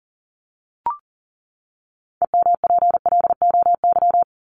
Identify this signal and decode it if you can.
Morse code, heard as a keyed tone, about 40 wpm, 700 Hz. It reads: EMPLOY
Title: ED Suffix - Top 100 Words with Base Words 40wpm